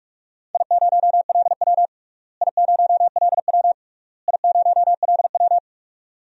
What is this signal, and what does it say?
Morse code, heard as a keyed tone, about 47 wpm, 700 Hz. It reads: I0LW I0LW I0LW
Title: DXCC-100 Call Signs - 4 Characters - Repeated 3x - 45wpm